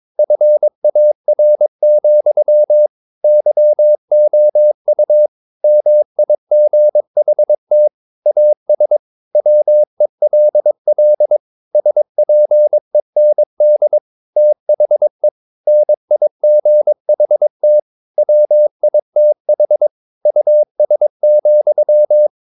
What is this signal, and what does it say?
Morse code, heard as a keyed tone, about 22 wpm, 600 Hz. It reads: FAR, YOU MIGHT AS WELL SPEND THE NIGHT WITH US,
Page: Romanen Never af Ken Follet på engelsk.